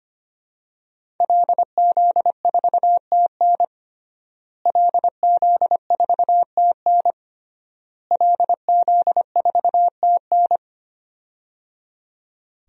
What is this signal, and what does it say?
Morse code, heard as a keyed tone, about 25 wpm, 700 Hz. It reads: LZ4TN LZ4TN LZ4TN
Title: DXCC-100 Call Signs - 5 Characters - Repeated 3x - 25wpm